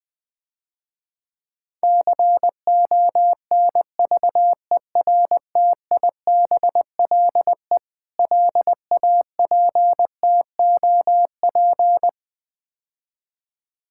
Morse code, heard as a keyed tone, about 20 wpm, 700 Hz. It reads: CONVERTIBLE LAPTOP
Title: Difficult Words - Speed-Racing with Hints - 20wpm